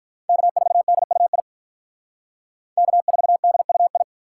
Morse code, heard as a keyed tone, about 46 wpm, 700 Hz. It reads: K4DUI K4DUI